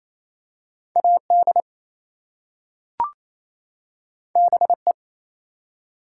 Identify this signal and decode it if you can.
Morse code, heard as a keyed tone, about 27 wpm, 700 Hz. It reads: AD BE
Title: Two Letter Words - 28wpm